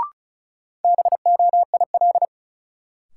Morse code, heard as a keyed tone, about 36 wpm, 700 Hz. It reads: BOIL
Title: Sets of 5 Words Encore - New Word Drill - 35wpm